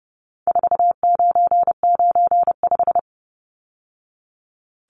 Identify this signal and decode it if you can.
Morse code, heard as a keyed tone, about 30 wpm, 700 Hz. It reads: 4995
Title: Numbers - 4-Digit 30wpm